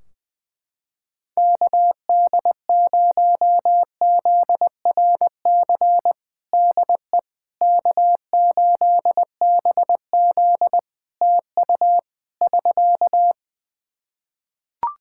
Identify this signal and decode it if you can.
Morse code, heard as a keyed tone, about 20 wpm, 700 Hz. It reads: KD0ZRC DE K8BZ TU <SK>
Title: Practice QSO - Exchange 4 - Ending - 20wpm